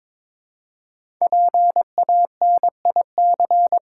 Morse code, heard as a keyed tone, about 23 wpm, 700 Hz. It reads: PANIC
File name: 5 Letter Words - First 3 Letters Given - 22wpm